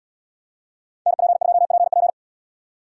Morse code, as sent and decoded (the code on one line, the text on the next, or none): -. .--. ..--- .--. .--.
NP2PP